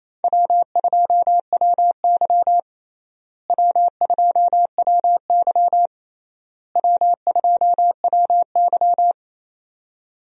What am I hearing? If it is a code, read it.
Morse code, heard as a keyed tone, about 28 wpm, 700 Hz. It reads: W2WY W2WY W2WY
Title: USA Call Signs - 4 Characters - Repeated 3x - 28wpm